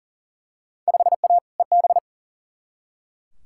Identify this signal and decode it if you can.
Morse code, heard as a keyed tone, about 40 wpm, 700 Hz. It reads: 5A EB
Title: Field Day Exchanges Encore - 40wpm